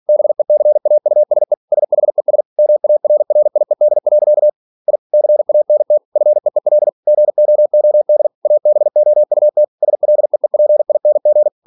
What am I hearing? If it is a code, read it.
Morse code, heard as a keyed tone, about 46 wpm, 600 Hz. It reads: BECAUSE SHES MARRIED. I CANT FEEL GOOD ABOUT SLEEPING
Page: Never 47wpm